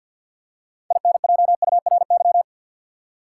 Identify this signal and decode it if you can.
Morse code, heard as a keyed tone, about 50 wpm, 700 Hz. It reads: INJURY
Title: Six Letter Words - 50wpm